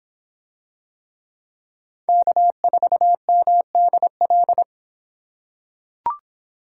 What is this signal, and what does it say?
Morse code, heard as a keyed tone, about 24 wpm, 700 Hz. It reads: K4MDL